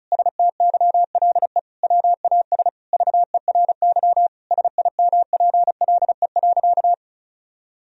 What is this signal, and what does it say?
Morse code, heard as a keyed tone, about 36 wpm, 700 Hz. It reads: STYLE WAS VERY SIMPLE.
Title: Sentences from Top 1000 Words - Review Part 4 - 35wpm